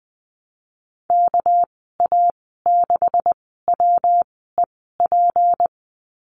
Morse code, heard as a keyed tone, about 20 wpm, spaced as for 14 wpm, 700 Hz. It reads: KA6WEP
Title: 238 Lesson 38g - Letter q - Call Signs 20wpm